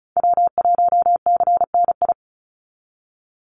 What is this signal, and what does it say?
Morse code, heard as a keyed tone, about 34 wpm, 700 Hz. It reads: W1CNI
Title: USA Call Signs 35wpm